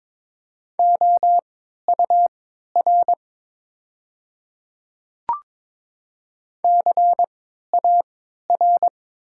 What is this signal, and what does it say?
Morse code, heard as a keyed tone, about 22 wpm, spaced as for 12 wpm, 700 Hz. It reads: OUR CAR
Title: 3x Character Spacing - Top 300 Words - 22wpm